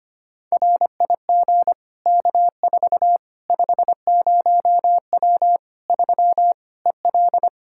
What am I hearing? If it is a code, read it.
Morse code, heard as a keyed tone, about 25 wpm, 700 Hz. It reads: RIG K4 50W 3 EL
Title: Practice QSO - Exchange 2 - Info - 25wpm